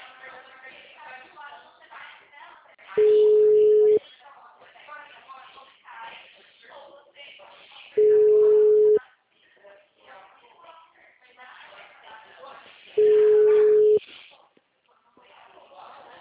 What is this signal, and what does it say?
Ringback tone (Europe, 425 Hz), with speech.